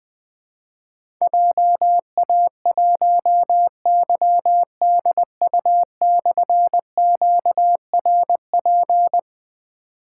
Morse code, heard as a keyed tone, about 20 wpm, 700 Hz. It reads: JA1YDU/QRP
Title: Difficult Call Signs 20wpm